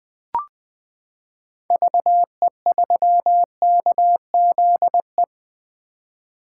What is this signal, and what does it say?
Morse code, heard as a keyed tone, about 20 wpm, 700 Hz. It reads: VE3KZE
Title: 2023 POTA Activator Call Signs - 20wpm